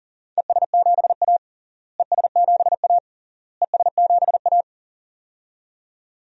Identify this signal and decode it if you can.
Morse code, heard as a keyed tone, about 40 wpm, 700 Hz. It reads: ES7A ES7A ES7A
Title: DXCC-100 Call Signs - 4 Characters - Repeated 3x - 40wpm